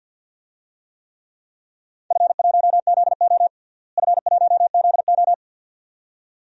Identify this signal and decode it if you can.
Morse code, heard as a keyed tone, about 52 wpm, 700 Hz. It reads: F1ZO F1ZO
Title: DXCC-100 Call Signs - 4 Characters - Repeated 2x - 50wpm